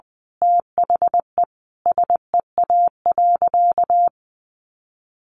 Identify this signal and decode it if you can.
Morse code, heard as a keyed tone, about 20 wpm, 700 Hz. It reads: THE SEA.
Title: Sentences from Top 1500 Words 20wpm